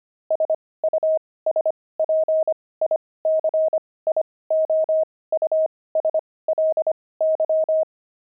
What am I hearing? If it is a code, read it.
Morse code, heard as a keyed tone, about 25 wpm, spaced as for 18 wpm, 630 Hz. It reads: SUSPICIOUSLY
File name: pangram-fs-18-cs-25-squirrel